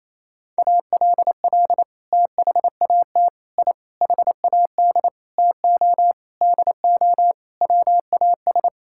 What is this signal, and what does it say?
Morse code, heard as a keyed tone, about 28 wpm, 690 Hz. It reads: ALL THAT I HAD TO DO WAS
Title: Sentences from Top 1000 Words - Review Part 6 - 28wpm